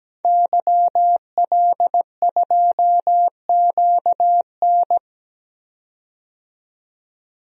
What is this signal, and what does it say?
Morse code, heard as a keyed tone, about 17 wpm, 700 Hz. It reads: YL2QN
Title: DXCC-100 Call Signs - 5 Characters - 17wpm